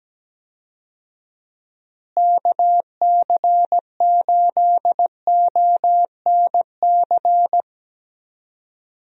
Morse code, heard as a keyed tone, about 17 wpm, 700 Hz. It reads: KC8ONC
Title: USA Call Signs - 6 Characters - Repeated 3x - 17wpm